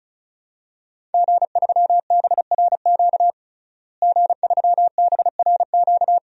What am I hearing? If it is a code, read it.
Morse code, heard as a keyed tone, about 36 wpm, 700 Hz. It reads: G3BRQ G3BRQ
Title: DXCC-100 Call Signs - 5 Characters - Repeated 2x - 35wpm